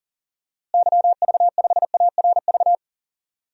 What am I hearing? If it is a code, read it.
Morse code, heard as a keyed tone, about 40 wpm, 700 Hz. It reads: YV5ARV
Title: DXCC-100 Call Signs - 6 Characters - Repeated 3x - 40wpm